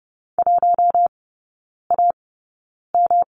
Morse code, heard as a keyed tone, about 30 wpm, spaced as for 9 wpm, 700 Hz. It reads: 1AM
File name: Lesson 18e - Letter m - Three Characters 30wpm